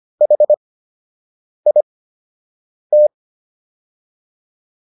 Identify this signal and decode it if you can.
Morse code, heard as a keyed tone, about 25 wpm, spaced as for 7 wpm, 600 Hz. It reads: HIT